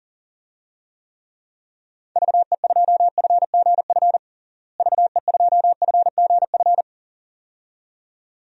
Morse code, heard as a keyed tone, about 40 wpm, 700 Hz. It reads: VE2FGF VE2FGF
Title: DXCC-100 Call Signs - 6 Characters - Repeated 2x - 40wpm